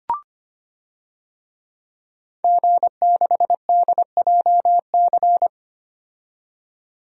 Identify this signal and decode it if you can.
Morse code, heard as a keyed tone, about 25 wpm, 700 Hz. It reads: G6DJC
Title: DXCC-100 Call Signs - 5 Characters - 25wpm